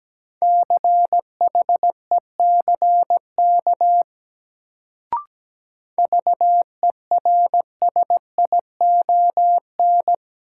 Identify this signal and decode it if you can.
Morse code, heard as a keyed tone, about 17 wpm, 700 Hz. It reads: CHECK VERSION
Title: Sentences from Top 1400 Words - New Word Drill Alt - 17wpm